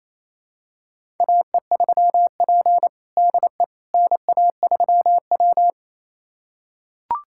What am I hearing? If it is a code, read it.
Morse code, heard as a keyed tone, about 28 wpm, 700 Hz. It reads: AE3P DE NA3W
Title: USA Callsign DE Callsign - 4 Characters - 28wpm